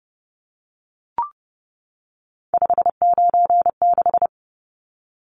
Morse code, heard as a keyed tone, about 30 wpm, 700 Hz. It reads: 596
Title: Numbers - 3-Digit 30wpm